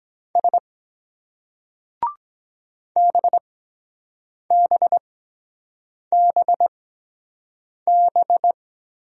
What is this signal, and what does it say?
Morse code, heard as a keyed tone, about 23 wpm, spaced as for 7 wpm, 700 Hz. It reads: S BBBB